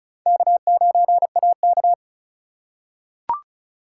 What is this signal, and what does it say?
Morse code, heard as a keyed tone, about 34 wpm, 700 Hz. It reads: K9AK
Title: USA Call Signs - 4 Characters - 4x Character Spacing - 35wpm